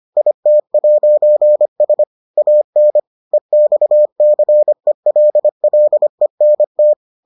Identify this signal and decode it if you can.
Morse code, heard as a keyed tone, about 25 wpm, 600 Hz. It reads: IT'S AN EXCELLENT